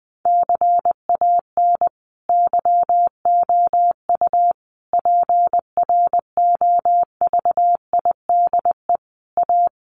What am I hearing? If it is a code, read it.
Morse code, heard as a keyed tone, about 20 wpm, 700 Hz. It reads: CAN YOU PROVIDE A
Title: Sentences from Top 1500 Words 20wpm